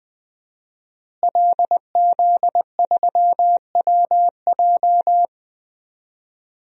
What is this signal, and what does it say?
Morse code, heard as a keyed tone, about 20 wpm, 700 Hz. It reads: LZ3WJ
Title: DXCC-100 Call Signs - 5 Characters - Repeated 2x - 20wpm